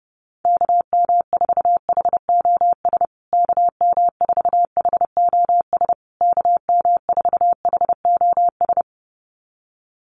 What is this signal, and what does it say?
Morse code, heard as a keyed tone, about 30 wpm, 700 Hz. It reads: KM4HOS KM4HOS KM4HOS
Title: USA Call Signs Repeated 3x 30wpm